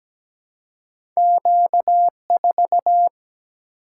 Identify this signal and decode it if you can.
Morse code, heard as a keyed tone, about 17 wpm, 700 Hz. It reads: Q 4
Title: Single-Letter Single-Number 17wpm